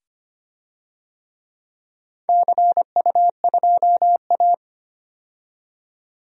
Morse code, heard as a keyed tone, about 25 wpm, 690 Hz. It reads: CU2A
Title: DXCC-100 Call Signs - 4 Characters - 25wpm